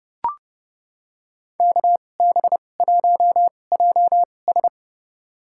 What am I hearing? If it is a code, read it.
Morse code, heard as a keyed tone, about 30 wpm, spaced as for 22 wpm, 700 Hz. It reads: KB1JS
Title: USA Call Signs - 5 Characters - 2x Character Spacing - 30wpm